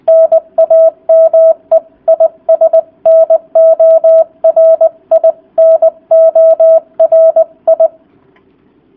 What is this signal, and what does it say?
Morse code, heard as a keyed tone, about 20 wpm, 660 Hz. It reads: NAME IS NORI NORI